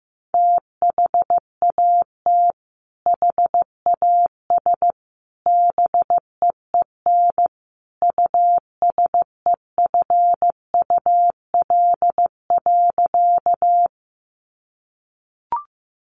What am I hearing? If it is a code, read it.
Morse code, heard as a keyed tone, about 15 wpm, 700 Hz. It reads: THAT HAS BEEN USEFUL.
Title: Sentences from Top 700 Words 15wpm